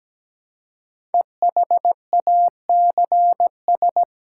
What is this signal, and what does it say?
Morse code, heard as a keyed tone, about 17 wpm, 700 Hz. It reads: EHACS
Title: Code Groups - 5 Letters - 17wpm